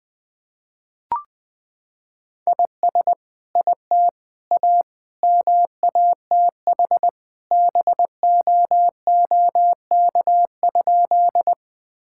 Morse code, heard as a keyed tone, about 20 wpm, 700 Hz. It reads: IS IT A MATH BOOK?
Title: Sentences from Top 1000 Words - Review Part 6 - 20wpm